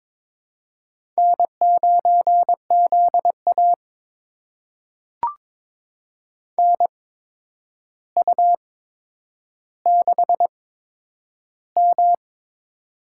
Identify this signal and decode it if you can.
Morse code, heard as a keyed tone, about 22 wpm, spaced as for 10 wpm, 700 Hz. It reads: N9ZA NU6M
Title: USA Call Signs - 4 Characters - 8x Character Spacing - 22wpm